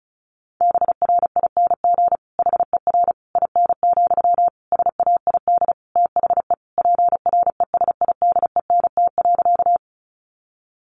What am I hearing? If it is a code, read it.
Morse code, heard as a keyed tone, about 34 wpm, 700 Hz. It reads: BRING HER IN, SAID THE PRESIDENT.
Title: Sentences from Top 1000 Words 35wpm